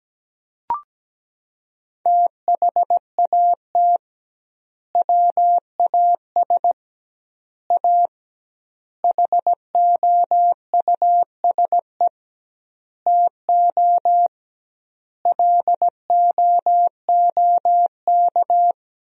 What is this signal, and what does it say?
Morse code, heard as a keyed tone, about 17 wpm, 700 Hz. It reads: THAT WAS A HOUSE TO LOOK
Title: 2x Word-Spacing - Sentences from Top 200 Words - 17wpm